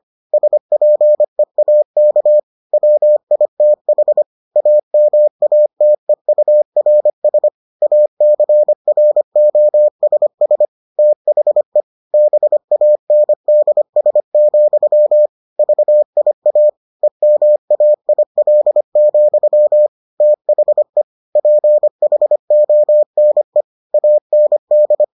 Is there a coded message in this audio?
Yes: Morse code from 0.3 to 25.0 s, heard as a keyed tone, about 25 wpm, 600 Hz. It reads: SPEAK WITH AMATEURS ACROSS THE BANDS, VIA EMAIL, THE PHONE AND